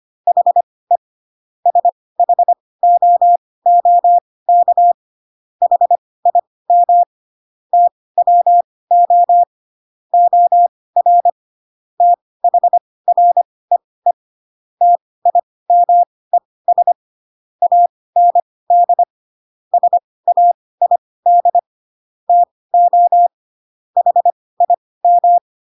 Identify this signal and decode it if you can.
Morse code, heard as a keyed tone, about 25 wpm, spaced as for 18 wpm, 700 Hz. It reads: HE SHOOK HIM TWO OR THREE TIMES AND SAID TO HIM
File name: Pinocchio ch3 12 25 ik1hge